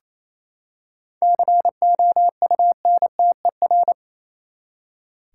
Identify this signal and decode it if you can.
Morse code, heard as a keyed tone, about 28 wpm, 700 Hz. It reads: COUNTER
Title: Seven Letter Words - 28wpm